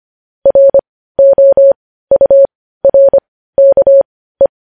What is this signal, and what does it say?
Morse code, heard as a keyed tone, about 25 wpm, spaced as for 15 wpm, 550 Hz. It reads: ROURKE
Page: news-headlines-in-morse-code-at-15-wpm